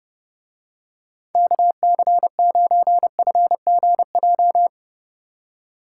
Morse code, heard as a keyed tone, about 29 wpm, 700 Hz. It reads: KC9FGJ